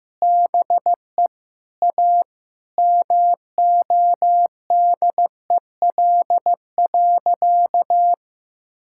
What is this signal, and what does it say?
Morse code, heard as a keyed tone, about 15 wpm, 700 Hz. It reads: BE A MODEL.
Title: Sentences from Top 1000 Words - Review Part 5 - 15wpm